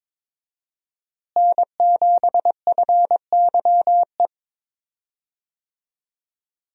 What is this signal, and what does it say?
Morse code, heard as a keyed tone, about 22 wpm, 700 Hz. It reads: N7FYE
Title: USA Call Signs - 5 Characters - Repeated 2x - 22wpm